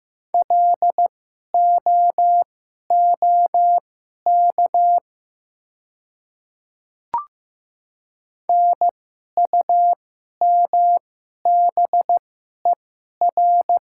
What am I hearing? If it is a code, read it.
Morse code, heard as a keyed tone, about 15 wpm, spaced as for 11 wpm, 700 Hz. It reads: LOOK NUMBER
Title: 2x Character Spacing - Top 100 Words - 15wpm